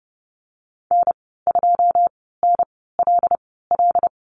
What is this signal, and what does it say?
Morse code, heard as a keyed tone, about 30 wpm, spaced as for 17 wpm, 700 Hz. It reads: N2NLL
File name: Lesson 14g - Number 2 - Call Signs 30wpm